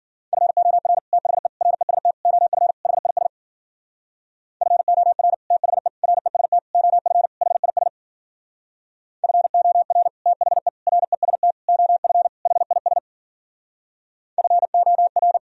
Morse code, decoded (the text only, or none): FOR THE REST OF HIS FOR THE REST OF HIS FOR THE REST OF HIS FOR